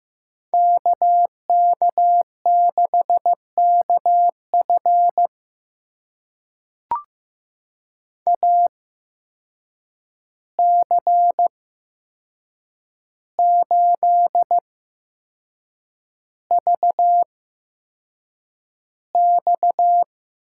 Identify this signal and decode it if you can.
Morse code, heard as a keyed tone, about 15 wpm, spaced as for 6 wpm, 700 Hz. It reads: KK6KF AC8VX
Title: USA Call Signs - 5 Characters - 8x Character Spacing - 15wpm